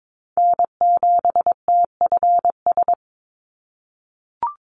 Morse code, heard as a keyed tone, about 22 wpm, 700 Hz. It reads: N7TFS
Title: USA Call Signs 22wpm